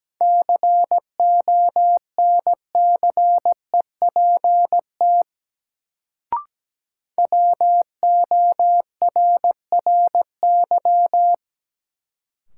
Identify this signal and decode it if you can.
Morse code, heard as a keyed tone, about 17 wpm, 700 Hz. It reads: CONCEPT WORRY
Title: Sentences from Top 1100 Words - New Word Drill - 17wpm